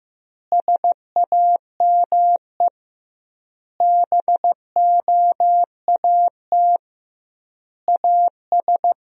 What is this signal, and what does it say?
Morse code, heard as a keyed tone, about 15 wpm, 700 Hz. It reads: SAME BOAT AS
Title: 2x Word-Spacing - Sentences from Top 500 Words - 15wpm